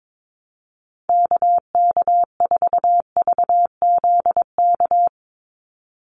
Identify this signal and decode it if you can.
Morse code, heard as a keyed tone, about 22 wpm, 700 Hz. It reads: KK4VZK
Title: USA Call Signs 22wpm